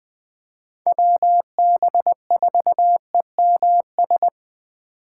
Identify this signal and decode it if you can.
Morse code, heard as a keyed tone, about 20 wpm, 700 Hz. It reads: WB4EMS
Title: USA Call Signs - 2x Character Spacing - 20wpm